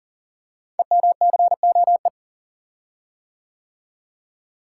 Morse code, heard as a keyed tone, about 42 wpm, 700 Hz. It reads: EMCOE
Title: Code Groups - 5 Letter:Numbers - 40wpm